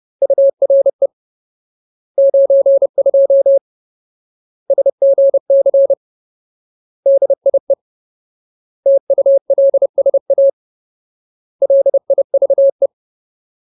Morse code, heard as a keyed tone, about 30 wpm, 550 Hz. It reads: URE 92 SGC DIE TULSA LIVE
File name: Ultimate Word List / 30WPM Ultimate